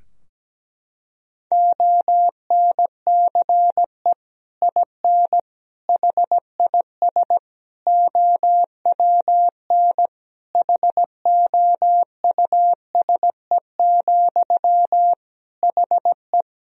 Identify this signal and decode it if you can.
Morse code, heard as a keyed tone, about 17 wpm, 700 Hz. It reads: ONCE IN HIS OWN HOUSE, HE
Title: Sentences from Top 1000 Words - Review Part 2 - 17wpm